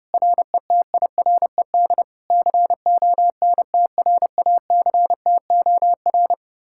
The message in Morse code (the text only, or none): RETIRED CONTRACTOR